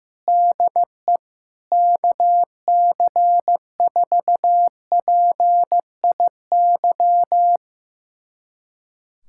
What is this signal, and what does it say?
Morse code, heard as a keyed tone, about 16 wpm, 700 Hz. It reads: DE KC4PIY
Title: USA Callsign DE Callsign - 6 Characters - 15wpm